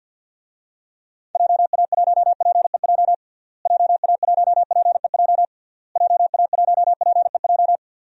Morse code, heard as a keyed tone, about 51 wpm, 700 Hz. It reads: JA1PEJ JA1PEJ JA1PEJ
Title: DXCC-100 Call Signs - 6 Characters - Repeated 3x - 50wpm